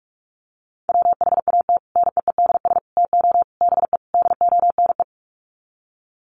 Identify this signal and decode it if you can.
Morse code, heard as a keyed tone, about 44 wpm, 700 Hz. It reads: WHAT NEEDS TO BE DONE
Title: Sets of 5 Words Encore 45wpm